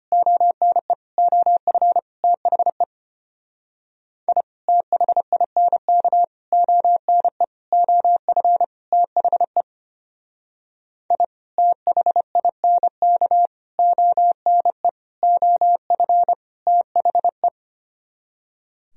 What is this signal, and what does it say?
Morse code, heard as a keyed tone, about 32 wpm, 700 Hz. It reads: ONE OF THE I THINK ONE OF THE I THINK ONE OF THE